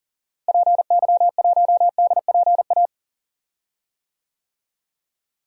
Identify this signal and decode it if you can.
Morse code, heard as a keyed tone, about 40 wpm, 700 Hz. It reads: PY1DPA
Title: DXCC-100 Call Signs - 6 Characters - Repeated 2x - 40wpm